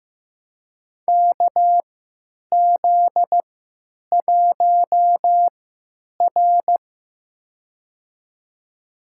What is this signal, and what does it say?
Morse code, heard as a keyed tone, about 15 wpm, spaced as for 9 wpm, 700 Hz. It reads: KZ1R
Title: USA Call Signs - 4 Characters - 3x Character Spacing - 15wpm